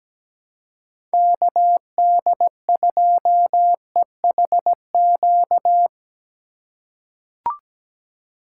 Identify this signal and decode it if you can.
Morse code, heard as a keyed tone, about 17 wpm, 700 Hz. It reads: KD2EHQ
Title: USA Call Signs - 6 Characters - 4x Character Spacing - 17wpm